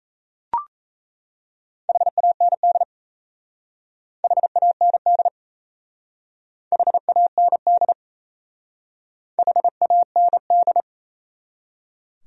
Morse code, heard as a keyed tone, about 34 wpm, 700 Hz. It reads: HAND HAND HAND HAND
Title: Speed-Racing Top 200 Words 28wpm